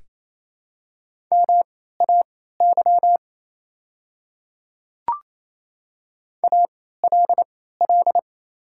Morse code, heard as a keyed tone, about 27 wpm, spaced as for 16 wpm, 700 Hz. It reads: MAY ALL
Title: 3x Character Spacing - Top 100 Words - 28wpm